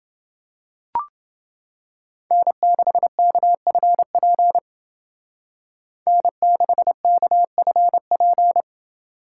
Morse code, heard as a keyed tone, about 28 wpm, 700 Hz. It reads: N6KFP N6KFP